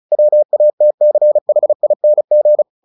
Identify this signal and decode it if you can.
Morse code, heard as a keyed tone, about 34 wpm, 600 Hz. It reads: WATCHING
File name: the old man and the sea 2 35wpm